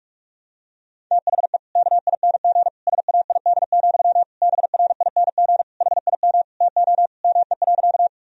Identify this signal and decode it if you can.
Morse code, heard as a keyed tone, about 44 wpm, 700 Hz. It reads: THE KING SAID, BRING HIM TO ME.
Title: Sentences from Top 1000 Words - Review Part 3 - 45wpm